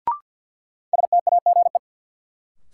Morse code, heard as a keyed tone, about 53 wpm, 700 Hz. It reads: STAGE
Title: Sentences from Top 1200 Words - New Word Drill - 50wpm